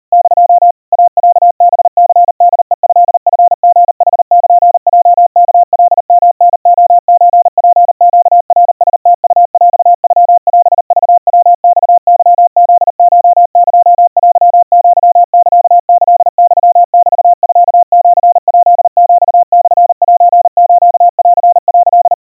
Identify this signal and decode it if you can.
Morse code (Japanese Wabun code), heard as a keyed tone, about 40 wpm, 700 Hz. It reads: <DO> イロハニホヘトチリヌルヲワカヨタレソツネナラムウヰノオクヤマケフコエテアサキユメミシヱヒモセスン」
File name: iroha 130 cpm 700 Hz Crispy